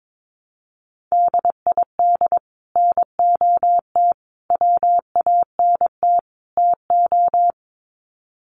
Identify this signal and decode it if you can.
Morse code, heard as a keyed tone, about 22 wpm, 700 Hz. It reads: DID NOT WANT TO
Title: Sets of 4 Words 22wpm